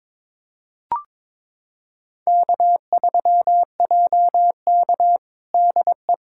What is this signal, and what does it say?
Morse code, heard as a keyed tone, about 22 wpm, 700 Hz. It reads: K3JK DE
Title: USA Callsign DE Callsign - 4 Characters - 22wpm